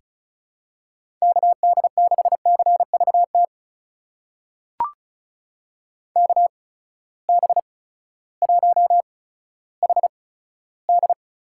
Morse code, heard as a keyed tone, about 36 wpm, spaced as for 16 wpm, 700 Hz. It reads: KD6CVT KB1HD
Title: USA Call Signs - 6 Characters - 8x Character Spacing - 35wpm